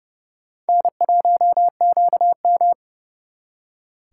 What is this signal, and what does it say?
Morse code, heard as a keyed tone, about 38 wpm, 700 Hz. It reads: N1QM